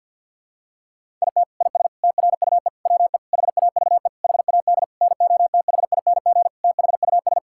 Morse code, heard as a keyed tone, about 48 wpm, 690 Hz. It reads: IT IS TRUE WE HAVE HAD NOTHING THUS
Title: Sentences from Top 1000 Words - Review Part 5 - 50wpm